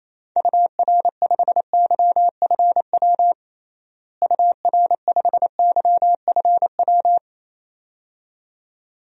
Morse code, heard as a keyed tone, about 27 wpm, 700 Hz. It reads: UR5YFW UR5YFW
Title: DXCC-100 Call Signs - 6 Characters - Repeated 2x - 28wpm